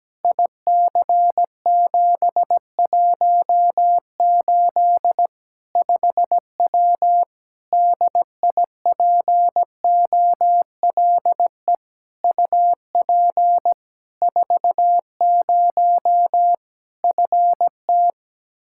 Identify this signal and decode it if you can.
Morse code, heard as a keyed tone, about 17 wpm, 700 Hz. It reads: IC718 5W DIPOLE UP 40 FT
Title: K1USN SST QSO Practice 17wpm